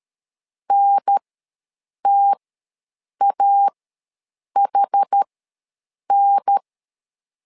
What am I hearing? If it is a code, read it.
Morse code, heard as a keyed tone, about 13 wpm, spaced as for 7 wpm, 800 Hz. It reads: NTAHN
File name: LEZIONI / 2 UHDN / 2 UHDN ws 12wpm 1708